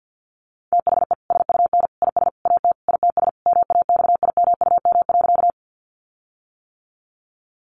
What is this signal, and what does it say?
Morse code, heard as a keyed tone, about 50 wpm, 700 Hz. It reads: THE SUN IS AT ITS MAXIMUM.
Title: Sentences from Top 1300 Words 50wpm